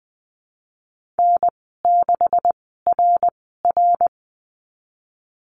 Morse code, heard as a keyed tone, about 20 wpm, spaced as for 15 wpm, 700 Hz. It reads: N6RR
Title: 147 Lesson 24g - Letter y - Call Signs 20wpm